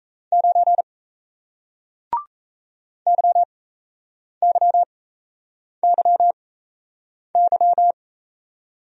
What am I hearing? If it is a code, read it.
Morse code, heard as a keyed tone, about 38 wpm, spaced as for 8 wpm, 700 Hz. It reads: 9 YYYY